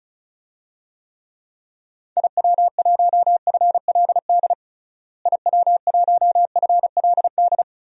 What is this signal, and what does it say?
Morse code, heard as a keyed tone, about 36 wpm, 700 Hz. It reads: IW1FLD IW1FLD
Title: DXCC-100 Call Signs - 6 Characters - Repeated 3x - 35wpm